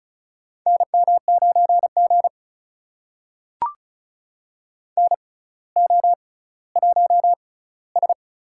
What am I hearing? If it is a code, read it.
Morse code, heard as a keyed tone, about 33 wpm, spaced as for 18 wpm, 700 Hz. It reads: NM9G NO1S
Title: USA Call Signs - 4 Characters - 6x Character Spacing - 35wpm